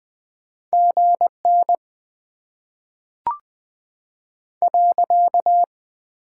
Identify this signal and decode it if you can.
Morse code, heard as a keyed tone, about 20 wpm, 700 Hz. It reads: GN .
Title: Top 90 QSO Elements - 6x Character Spacing - 20wpm